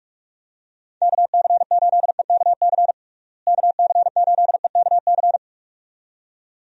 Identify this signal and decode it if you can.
Morse code, heard as a keyed tone, about 45 wpm, 700 Hz. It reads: KC8EKC KC8EKC
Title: USA Call Signs - 6 Characters - Repeated 2x - 45wpm